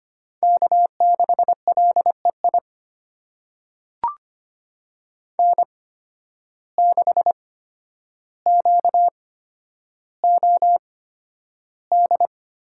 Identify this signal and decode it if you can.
Morse code, heard as a keyed tone, about 25 wpm, spaced as for 11 wpm, 700 Hz. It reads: K6LEI N6QOD
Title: USA Call Signs - 5 Characters - 8x Character Spacing - 25wpm